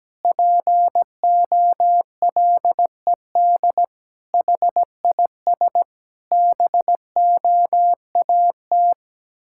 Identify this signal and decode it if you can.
Morse code, heard as a keyed tone, about 17 wpm, 700 Hz. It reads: POLED HIS BOAT
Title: Pangram Challenge v2 - 17wpm